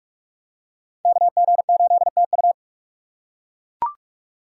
Morse code, heard as a keyed tone, about 44 wpm, 700 Hz. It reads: KG8TU